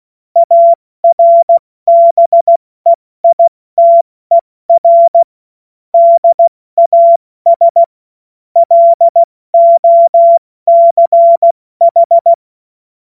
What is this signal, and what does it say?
Morse code, heard as a keyed tone, about 16 wpm, 680 Hz. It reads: ARBEITER DAS LOCH
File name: Morsekurs / Witze / Witz 22. August 2023 70 F80